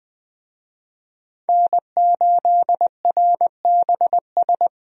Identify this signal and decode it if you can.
Morse code, heard as a keyed tone, about 20 wpm, 700 Hz. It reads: N8RBS
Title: USA Call Signs - 5 Characters - 20wpm